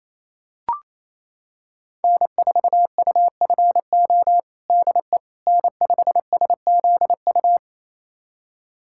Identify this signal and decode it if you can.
Morse code, heard as a keyed tone, about 28 wpm, 700 Hz. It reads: N4UFO DE N5SZU
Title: USA Callsign DE Callsign - 5 Characters - 28wpm